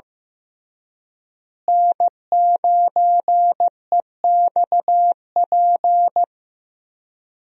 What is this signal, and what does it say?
Morse code, heard as a keyed tone, about 15 wpm, 700 Hz. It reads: N9EXP
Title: USA Call Signs - 5 Characters - Repeated 3x - 15wpm